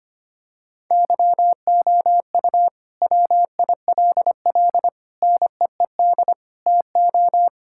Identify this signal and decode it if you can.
Morse code, heard as a keyed tone, about 25 wpm, 700 Hz. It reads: YOU WILL NEED TO
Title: Sentences from Top 1000 Words - Review Part 5 - 25wpm